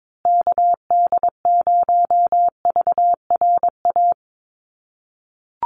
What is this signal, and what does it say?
Morse code, heard as a keyed tone, about 22 wpm, 700 Hz. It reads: KD0VRA
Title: USA Call Signs Repeated 2x 22wpm